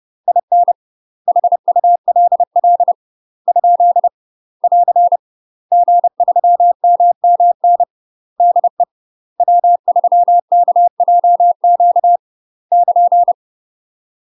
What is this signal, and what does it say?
Morse code, heard as a keyed tone, about 30 wpm, 700 Hz. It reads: IN HULL ? <AR> G3MMN DE W3KJQ <KN>
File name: UK MASTER QSOs TRAINING / 30WPM / UK MASTER QSOs 0035